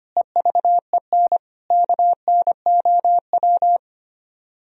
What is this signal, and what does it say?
Morse code, heard as a keyed tone, about 26 wpm, 700 Hz. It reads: EVEN KNOW
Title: Speed-Racing - Sets of 5 Words 25wpm